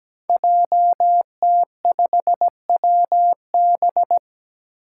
Morse code, heard as a keyed tone, about 17 wpm, 700 Hz. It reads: JT5WB
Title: Five Challenging Letter-Number Combinations 17wpm